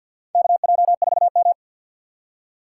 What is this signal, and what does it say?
Morse code, heard as a keyed tone, about 54 wpm, 700 Hz. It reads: KJ4M
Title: USA Call Signs - 4 Characters - 6x Character Spacing - 50wpm